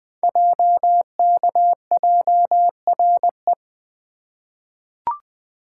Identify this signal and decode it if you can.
Morse code, heard as a keyed tone, about 20 wpm, 700 Hz. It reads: JKJRE
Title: Code Groups - 5 Letters - 20wpm